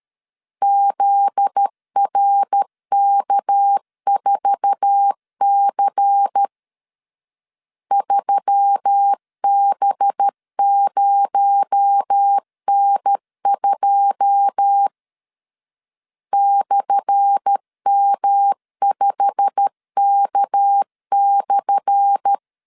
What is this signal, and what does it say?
Morse code, heard as a keyed tone, about 13 wpm, 800 Hz. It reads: ZRK4C 3B0N2 /M5K/
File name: LEZIONI / 10 SIGNS / 10A SLASH 12wpm 2314